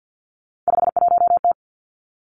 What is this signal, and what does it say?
Morse code, heard as a keyed tone, about 50 wpm, 700 Hz. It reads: 51T
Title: Three Challenging Letter-Number Combinations 50wpm